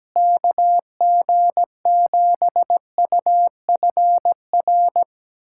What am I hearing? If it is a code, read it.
Morse code, heard as a keyed tone, about 17 wpm, 700 Hz. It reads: KG7UFR
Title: USA Callsign DE Callsign 17wpm